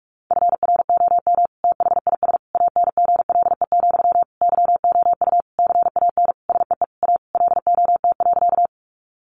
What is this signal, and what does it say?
Morse code, heard as a keyed tone, about 44 wpm, 700 Hz. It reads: FROM THIS ANGLE, YOU CAN SEE A LOT.
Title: Sentences from Top 1500 Words 45wpm